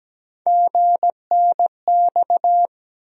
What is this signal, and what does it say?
Morse code, heard as a keyed tone, about 17 wpm, 700 Hz. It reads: GNX
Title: USA Call Sign Postfixes - 3 Letters 17wpm